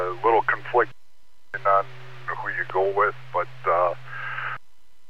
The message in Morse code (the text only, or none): none